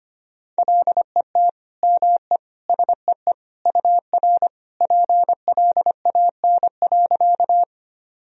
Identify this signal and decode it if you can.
Morse code, heard as a keyed tone, about 25 wpm, 700 Hz. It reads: LET ME SEE UR PLAN.
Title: Sentences from Top 500 Words and Abbreviations 25wpm